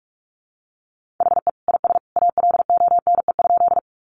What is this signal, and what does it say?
Morse code, heard as a keyed tone, about 44 wpm, 700 Hz. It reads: HE IS ALONE?
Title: Sentences from Top 1000 Words 45wpm